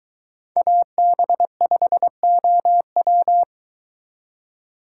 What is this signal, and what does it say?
Morse code, heard as a keyed tone, about 22 wpm, 700 Hz. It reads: AB5OW